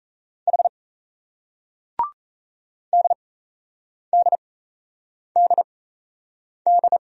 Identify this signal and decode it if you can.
Morse code, heard as a keyed tone, about 38 wpm, spaced as for 8 wpm, 700 Hz. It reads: H DDDD